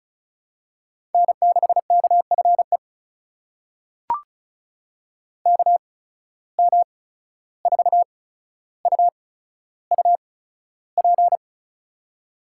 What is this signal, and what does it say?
Morse code, heard as a keyed tone, about 34 wpm, spaced as for 14 wpm, 700 Hz. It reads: N6KFE KM4UUP
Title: USA Call Signs - 8x Character Spacing - 35wpm